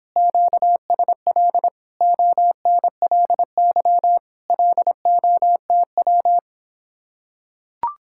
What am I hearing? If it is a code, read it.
Morse code, heard as a keyed tone, about 26 wpm, 700 Hz. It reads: QSL ONLY LOTW